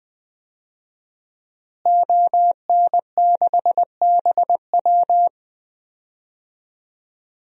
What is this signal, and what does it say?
Morse code, heard as a keyed tone, about 20 wpm, 700 Hz. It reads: ON6BW
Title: DXCC-100 Call Signs - 5 Characters - Repeated 2x - 20wpm